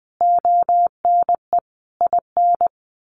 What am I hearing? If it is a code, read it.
Morse code, heard as a keyed tone, about 20 wpm, 700 Hz. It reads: ONE IN
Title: A Sound Of Thunder 20wpm